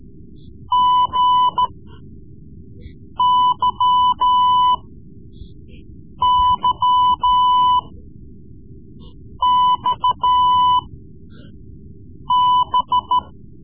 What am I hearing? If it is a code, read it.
Morse code, heard as a keyed tone, about 14 wpm, spaced as for 5 wpm, 1000 Hz. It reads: GYYXB
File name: Morse Lessons / Morse Lessons / Tape-3B